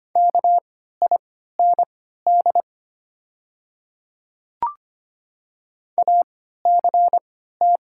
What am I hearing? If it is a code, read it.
Morse code, heard as a keyed tone, about 25 wpm, spaced as for 14 wpm, 690 Hz. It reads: KIND ACT
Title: 3x Character Spacing - Top 300 Words - 25wpm